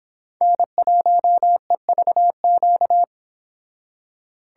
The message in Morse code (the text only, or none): N1EVQ